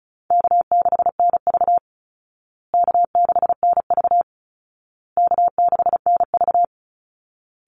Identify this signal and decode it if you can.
Morse code, heard as a keyed tone, about 34 wpm, 700 Hz. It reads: K6NV K6NV K6NV
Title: Easy Call Signs Repeated 3x - 35wpm